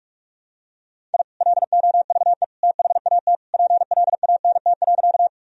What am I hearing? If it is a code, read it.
Morse code, heard as a keyed tone, about 46 wpm, 700 Hz. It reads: I LOVE THAT PLANT.
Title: Sentences from Top 1000 Words - Review Part 1 - 45wpm